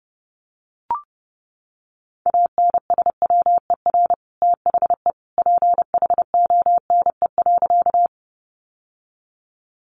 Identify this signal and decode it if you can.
Morse code, heard as a keyed tone, about 30 wpm, 700 Hz. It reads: ANSWER THE PHONE.
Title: Sentences from Top 1100 Words 30wpm